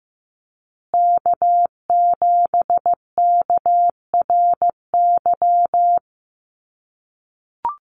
Morse code, heard as a keyed tone, about 15 wpm, 700 Hz. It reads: K7KRY
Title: USA Call Signs Repeated 2x 15wpm